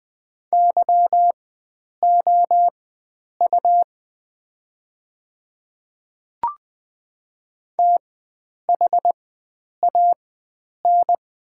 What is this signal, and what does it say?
Morse code, heard as a keyed tone, about 20 wpm, spaced as for 9 wpm, 700 Hz. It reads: YOU THAN
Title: 4x Character Spacing - Top 100 Words - 20wpm